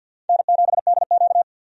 Morse code, heard as a keyed tone, about 48 wpm, 700 Hz. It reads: N7DQ
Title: USA Call Signs - 4 Characters - 50wpm